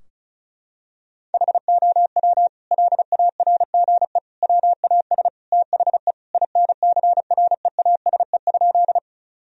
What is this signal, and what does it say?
Morse code, heard as a keyed tone, about 34 wpm, 700 Hz. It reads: HOW LARGE WAS THE INCREASE?
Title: Sentences from Top 1000 Words - Review Part 4 - 35wpm